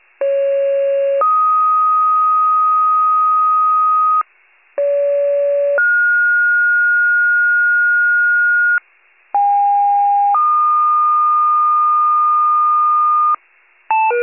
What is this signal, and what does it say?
Two-tone page, heard as a one-second tone, then a three-second tone, about 550 then 1250 Hz, about 570 then 1430 Hz, about 810 then 1220 Hz. It is heard from 0.2 to 13.3 s.
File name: pager / 29 FIRE / wcalert